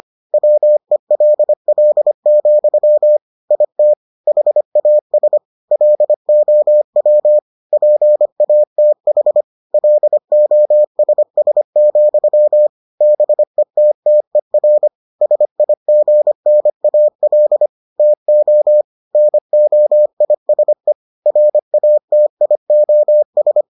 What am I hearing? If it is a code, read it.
Morse code, heard as a keyed tone, about 25 wpm, 600 Hz. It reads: WELL, IT HAS LOW PATH LOSS, BETTER SIGNAL TO NOISE RATIOS